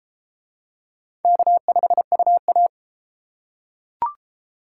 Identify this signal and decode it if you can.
Morse code, heard as a keyed tone, about 31 wpm, 700 Hz. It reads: K5UA